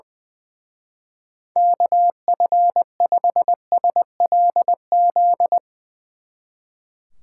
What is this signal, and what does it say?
Morse code, heard as a keyed tone, about 20 wpm, 700 Hz. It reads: KF5SLZ
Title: Speed Racing - USA 2x3 Call Signs 20wpm